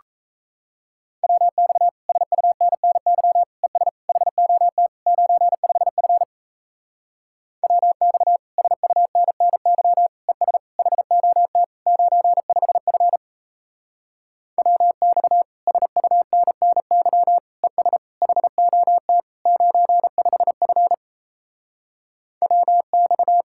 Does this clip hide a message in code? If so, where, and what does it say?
Yes: Morse code from 1.2 to 23.4 s, heard as a keyed tone, about 38 wpm, 700 Hz. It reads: WX SUNNY ES HOT 95F WX SUNNY ES HOT 95F WX SUNNY ES HOT 95F WX